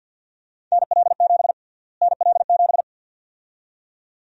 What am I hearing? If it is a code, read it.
Morse code, heard as a keyed tone, about 52 wpm, 700 Hz. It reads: NL7 NL7
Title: USA Call Sign Prefixes - Repeated 2x - 50wpm